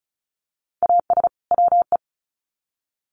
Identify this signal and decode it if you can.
Morse code, heard as a keyed tone, about 34 wpm, 700 Hz. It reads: AS WE
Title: Sets of 2 Words 35wpm